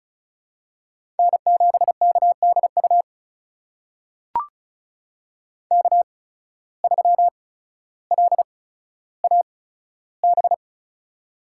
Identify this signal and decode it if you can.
Morse code, heard as a keyed tone, about 36 wpm, spaced as for 15 wpm, 700 Hz. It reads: N7KDU K3LAB
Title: USA Call Signs - 5 Characters - 8x Character Spacing - 35wpm